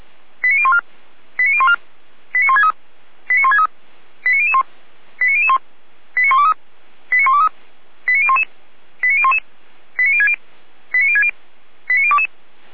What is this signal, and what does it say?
Five-tone selective call (ZVEI), heard as quick runs of five tones, tones of about 70 ms. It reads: 89014 89014 89163 89163 89001 89001 89123 89123 89011 89011 89060 89060 89022
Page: SDR/ZVEI